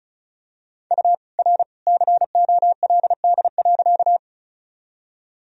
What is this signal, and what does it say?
Morse code, heard as a keyed tone, about 36 wpm, 700 Hz. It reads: U R COLD.
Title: Sentences from Top 400 Words and Abbreviations 35wpm